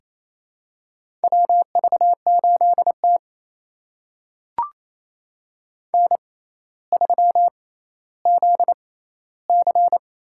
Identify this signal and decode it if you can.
Morse code, heard as a keyed tone, about 27 wpm, spaced as for 14 wpm, 700 Hz. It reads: WV8T N3ZC
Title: USA Call Signs - 4 Characters - 6x Character Spacing - 28wpm